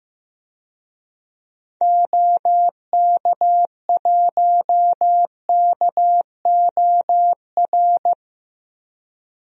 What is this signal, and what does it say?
Morse code, heard as a keyed tone, about 15 wpm, 700 Hz. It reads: OK1KOR
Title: DXCC-100 Call Signs - 6 Characters - 15wpm